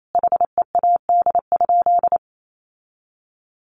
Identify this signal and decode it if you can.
Morse code, heard as a keyed tone, about 28 wpm, 700 Hz. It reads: HEAD?
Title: Sentences from Top 600 Words 28wpm